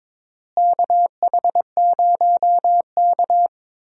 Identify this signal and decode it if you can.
Morse code, heard as a keyed tone, about 22 wpm, 700 Hz. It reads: KH0K
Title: USA Call Signs - 4 Characters - Repeated 2x - 22wpm